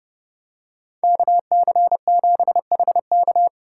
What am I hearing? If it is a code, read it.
Morse code, heard as a keyed tone, about 30 wpm, 700 Hz. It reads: KC7HK